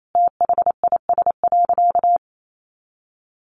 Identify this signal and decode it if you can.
Morse code, heard as a keyed tone, about 28 wpm, 700 Hz. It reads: THIS.
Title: Sentences from Top 100 Words 28wpm